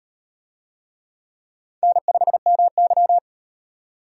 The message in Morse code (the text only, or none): N5MY